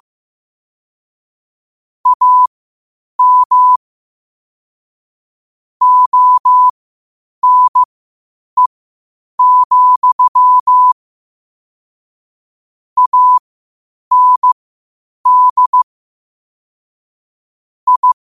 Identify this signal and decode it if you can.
Morse code, heard as a keyed tone, about 15 wpm, spaced as for 8 wpm, 1000 Hz. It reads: AM ONE, AND I
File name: qotdfiles / 8wpm